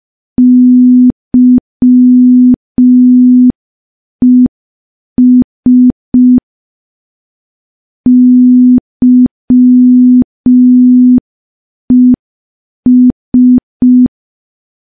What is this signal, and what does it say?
Morse code, heard as a keyed tone, about 5 wpm, 250 Hz. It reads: YES YES
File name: morse / morse-training-05WPM